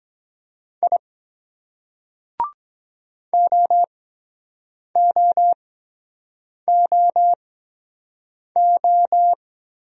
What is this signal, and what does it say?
Morse code, heard as a keyed tone, about 23 wpm, spaced as for 7 wpm, 700 Hz. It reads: I OOOO